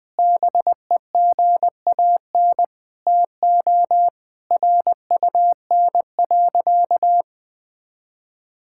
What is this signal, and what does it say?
Morse code, heard as a keyed tone, about 20 wpm, 700 Hz. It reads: BEGAN TO RUN.
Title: Sentences from Top 1600 Words 20wpm